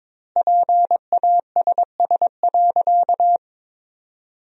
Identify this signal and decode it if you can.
Morse code, heard as a keyed tone, about 22 wpm, 690 Hz. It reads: PASS.
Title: 4x Word-Spacing - Sentences from Top 400 Words - 22wpm